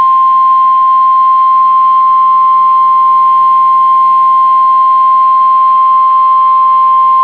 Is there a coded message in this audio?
Yes: Emergency Alert System (SAME) tones (the 1050 Hz attention tone) from 0.4 to 6.3 s.